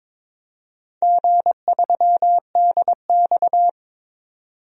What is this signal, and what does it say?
Morse code, heard as a keyed tone, about 22 wpm, 690 Hz. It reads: G3DX
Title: DXCC-100 Call Signs - 4 Characters - 22wpm